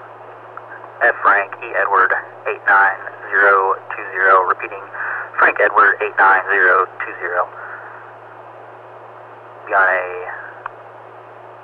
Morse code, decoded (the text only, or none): none